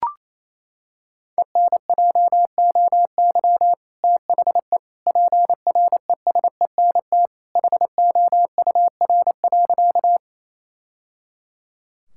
Morse code, heard as a keyed tone, about 27 wpm, 700 Hz. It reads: ENJOY THE PRESENT HOUR.
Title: Sentences from Top 1000 Words - Review Part 5 - 28wpm